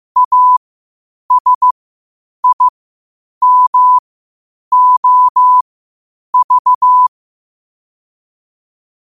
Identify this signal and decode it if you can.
Morse code, heard as a keyed tone, about 15 wpm, spaced as for 8 wpm, 1000 Hz. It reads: ASIMOV
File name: 8wpm